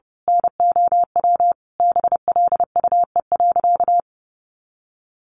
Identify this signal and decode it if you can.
Morse code, heard as a keyed tone, about 30 wpm, 700 Hz. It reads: NOW BLUE.
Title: A Sound Of Thunder 30wpm